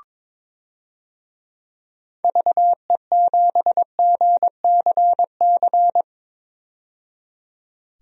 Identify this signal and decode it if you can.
Morse code, heard as a keyed tone, about 22 wpm, 700 Hz. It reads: VE7GCC
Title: DXCC-100 Call Signs - 6 Characters - 22wpm